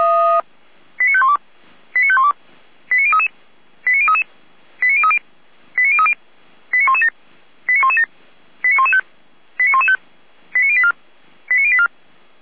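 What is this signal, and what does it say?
Five-tone selective call (ZVEI), heard as quick runs of five tones, tones of about 70 ms. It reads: 89512 89512 89033 89033 89030 89030 89117 89117 89115 89115 89084 89084
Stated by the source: ZVEI five-tone call received by SDR.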